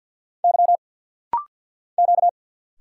Morse code, heard as a keyed tone, about 50 wpm, 700 Hz. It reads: Y Q
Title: Single Letters - Rapid-Fire 50wpm